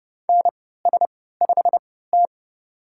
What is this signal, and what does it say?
Morse code, heard as a keyed tone, about 32 wpm, spaced as for 17 wpm, 700 Hz. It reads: NS5T
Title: USA Call Signs - 3x Character Spacing - 30wpm